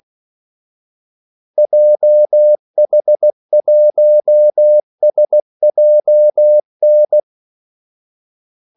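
Morse code, heard as a keyed tone, about 16 wpm, 600 Hz. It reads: JH1SJN
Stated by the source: Morse call sign practice at 16 wpm.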